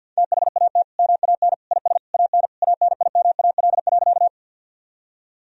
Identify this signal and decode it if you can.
Morse code, heard as a keyed tone, about 51 wpm, 700 Hz. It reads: THAT MAN IS AN ANIMAL.
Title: Sentences from Top 1000 Words - Review Part 1 - 50wpm